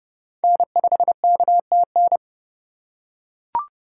Morse code, heard as a keyed tone, about 30 wpm, 700 Hz. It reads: N5KTN